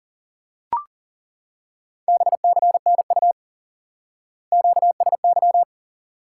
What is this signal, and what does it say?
Morse code, heard as a keyed tone, about 37 wpm, 700 Hz. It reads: BCNU QSY
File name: IQE Course 3 - Lesson 89 - YAGI - Two Elements - 40wpm